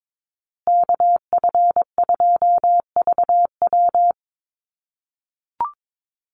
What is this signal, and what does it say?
Morse code, heard as a keyed tone, about 22 wpm, 700 Hz. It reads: KF2VW
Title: USA Call Signs Repeated 3x 22wpm